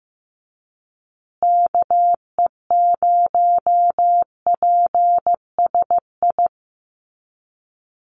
Morse code, heard as a keyed tone, about 15 wpm, 700 Hz. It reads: KE0PSI
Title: USA Call Signs Repeated 2x 15wpm